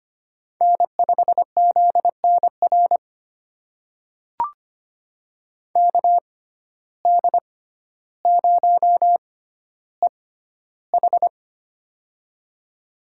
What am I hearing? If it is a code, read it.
Morse code, heard as a keyed tone, about 25 wpm, spaced as for 13 wpm, 700 Hz. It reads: N5ZNR KD0EH
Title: USA Call Signs - 5 Characters - 6x Character Spacing - 25wpm